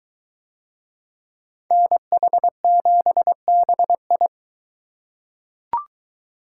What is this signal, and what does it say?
Morse code, heard as a keyed tone, about 22 wpm, 700 Hz. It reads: NH7BI